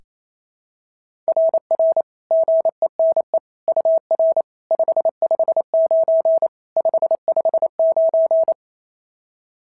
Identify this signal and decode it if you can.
Morse code, heard as a keyed tone, about 28 wpm, 660 Hz. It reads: RR GENE UR 559 559
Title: SOTA QSO Practice 28wpm